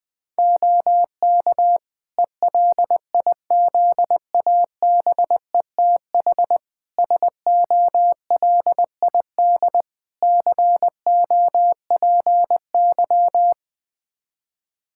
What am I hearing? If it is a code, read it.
Morse code, heard as a keyed tone, about 21 wpm, 700 Hz. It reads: OK ELIZABETH SOLID COPY
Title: Practice QSO - Exchange 3 - More Info - 20wpm